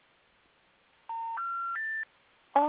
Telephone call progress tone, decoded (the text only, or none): special information tone